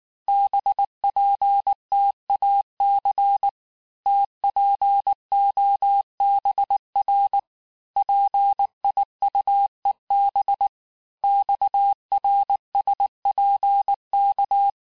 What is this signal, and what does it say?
Morse code, heard as a keyed tone, about 19 wpm, 800 Hz. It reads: BPTAC TPOBR PIUEB XRSPK
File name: cours-du-lundi files / cours-1583--texte-chiffre-18-mpm